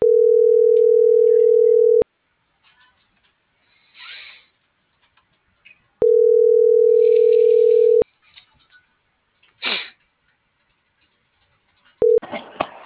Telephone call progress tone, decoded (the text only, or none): ringback tone